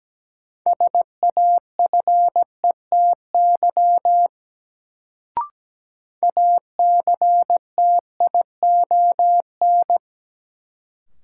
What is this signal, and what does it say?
Morse code, heard as a keyed tone, about 17 wpm, 700 Hz. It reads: SAFETY ACTION
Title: Sentences from Top 700 Words - New Word Drill - 17wpm